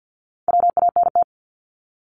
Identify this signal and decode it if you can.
Morse code, heard as a keyed tone, about 50 wpm, 700 Hz. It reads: WANT
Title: Multi-letter Phonemes Repeated 3x 50wpm